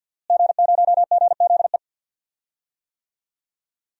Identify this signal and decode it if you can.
Morse code, heard as a keyed tone, about 47 wpm, 700 Hz. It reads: G0GZE
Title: DXCC-100 Call Signs - 5 Characters - Repeated 3x - 50wpm